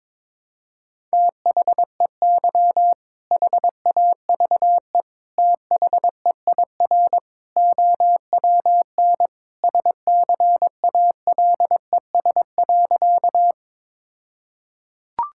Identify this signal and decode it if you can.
Morse code, heard as a keyed tone, about 21 wpm, 700 Hz. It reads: THEY HAVE THEIR OWN SCALES.
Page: morse-code-ninja-practice-22wpm